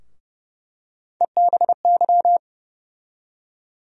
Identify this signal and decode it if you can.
Morse code, heard as a keyed tone, about 30 wpm, 700 Hz. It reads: EBY
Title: USA Call Sign Postfixes - 3 Letters 30wpm